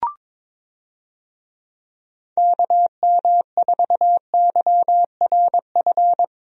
Morse code, heard as a keyed tone, about 21 wpm, 700 Hz. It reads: KM4YRF
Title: USA Call Signs - 6 Characters - Repeated 3x - 22wpm